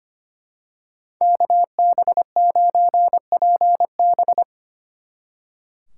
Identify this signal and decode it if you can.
Morse code, heard as a keyed tone, about 29 wpm, 700 Hz. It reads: KB9PB
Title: Speed Racing - USA 2x2 Call Signs 25wpm